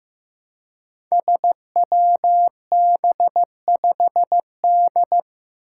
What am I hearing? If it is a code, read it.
Morse code, heard as a keyed tone, about 15 wpm, 700 Hz. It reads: SWB5D
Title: Five Challenging Letter-Number Combinations 15wpm